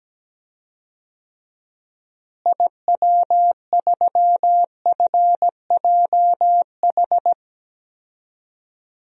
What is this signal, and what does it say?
Morse code, heard as a keyed tone, about 17 wpm, 700 Hz. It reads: IW3FJH
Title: DXCC-100 Call Signs - 6 Characters - Repeated 3x - 17wpm